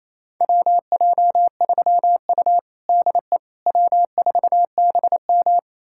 Morse code, heard as a keyed tone, about 28 wpm, 700 Hz. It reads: WJ3U DE W4BM
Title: USA Callsign DE Callsign - 4 Characters - 28wpm